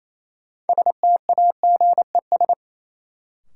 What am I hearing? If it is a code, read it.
Morse code, heard as a keyed tone, about 28 wpm, 700 Hz. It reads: STAGES
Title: Sets of 5 Words Encore - New Word Drill - 28wpm